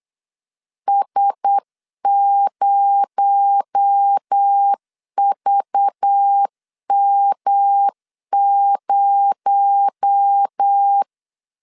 Morse code, heard as a keyed tone, about 8 wpm, 800 Hz. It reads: S0VM0
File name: CW / DET / 1 6 1500C S B 8wpm 5142